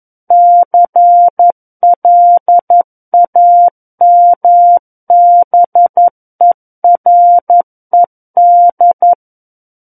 Morse code, heard as a keyed tone, about 11 wpm, 700 Hz. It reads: CLAMBERED
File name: 11 WPM